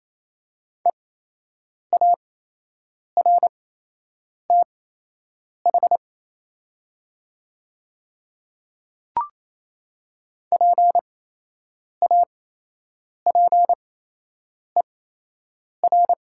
Morse code, heard as a keyed tone, about 28 wpm, spaced as for 8 wpm, 700 Hz. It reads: EARTH PAPER
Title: 8x Character Spacing - Top 300 Words - 28wpm